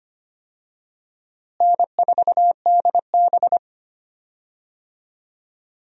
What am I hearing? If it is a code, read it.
Morse code, heard as a keyed tone, about 25 wpm, 700 Hz. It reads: N4DB
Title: USA Call Signs - 4 Characters - Repeated 2x - 25wpm